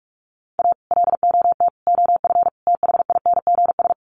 Morse code, heard as a keyed tone, about 44 wpm, 700 Hz. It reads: A LOT OF THINGS
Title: Sets of 4 Words 45wpm